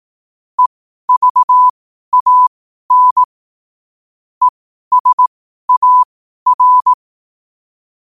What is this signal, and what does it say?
Morse code, heard as a keyed tone, about 18 wpm, spaced as for 12 wpm, 1000 Hz. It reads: EVAN ESAR
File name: qotdfiles / 12wpm